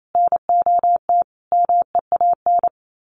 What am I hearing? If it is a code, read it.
Morse code, heard as a keyed tone, about 28 wpm, 700 Hz. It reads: NOT MEAN
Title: Sets of 5 Words 28wpm